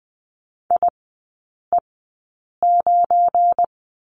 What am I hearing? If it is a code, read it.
Morse code, heard as a keyed tone, about 20 wpm, spaced as for 8 wpm, 700 Hz. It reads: IE9
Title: 178 Lesson 29e - Slash - Three Characters 20wpm